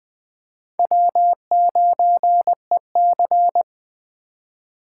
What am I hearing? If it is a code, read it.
Morse code, heard as a keyed tone, about 20 wpm, 700 Hz. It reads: W9EC
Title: Speed Racing - USA 1x2 Call Signs 17wpm